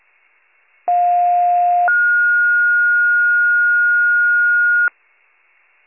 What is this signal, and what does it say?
Two-tone page, heard as a one-second tone, then a three-second tone, about 710 then 1430 Hz. It is heard from 0.9 to 4.9 s.